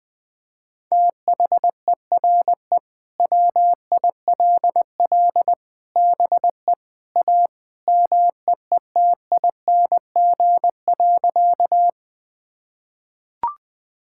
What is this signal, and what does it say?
Morse code, heard as a keyed tone, about 21 wpm, 700 Hz. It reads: THERE WILL BE A MEETING.
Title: Sentences with 3-Word Phrases - 20wpm